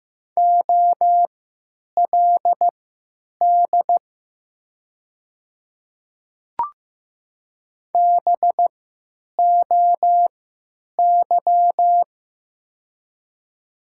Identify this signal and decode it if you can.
Morse code, heard as a keyed tone, about 15 wpm, spaced as for 9 wpm, 700 Hz. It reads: OLD BOY
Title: 3x Character Spacing - Top 200 Words - 15wpm